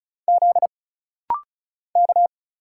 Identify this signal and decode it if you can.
Morse code, heard as a keyed tone, about 36 wpm, 700 Hz. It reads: Z K
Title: Single Letters - Rapid-Fire 35wpm